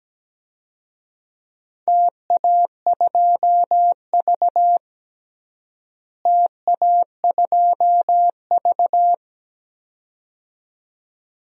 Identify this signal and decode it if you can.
Morse code, heard as a keyed tone, about 17 wpm, 700 Hz. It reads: TA2V TA2V
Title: DXCC-100 Call Signs - 4 Characters - Repeated 2x - 17wpm